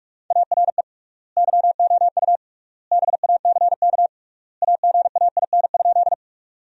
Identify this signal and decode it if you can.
Morse code, heard as a keyed tone, about 45 wpm, 700 Hz. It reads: ARE YOU BACK AGAIN?
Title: 3x Word-Spacing - Sentences from Top 200 Words - 45wpm